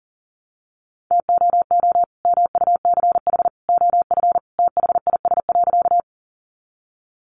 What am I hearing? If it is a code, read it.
Morse code, heard as a keyed tone, about 40 wpm, 700 Hz. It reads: TOO MUCH OF THIS.
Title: Sentences from Top 200 Words 40wpm